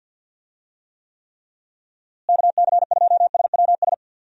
Morse code, heard as a keyed tone, about 50 wpm, 700 Hz. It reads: KC2SWS
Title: USA Call Signs - 6 Characters - Repeated 3x - 50wpm